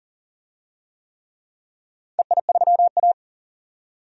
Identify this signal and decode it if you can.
Morse code, heard as a keyed tone, about 40 wpm, 700 Hz. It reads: EI3A
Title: DXCC-100 Call Signs - 4 Characters - 40wpm